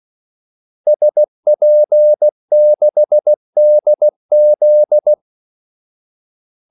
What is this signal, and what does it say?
Morse code, heard as a keyed tone, about 16 wpm, 600 Hz. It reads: SP6DZ
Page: Kaldesignaler 16 wpm